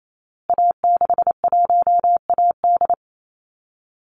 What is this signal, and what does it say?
Morse code, heard as a keyed tone, about 28 wpm, 700 Hz. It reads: A61AD
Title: Call Signs Repeated 2x - 28wpm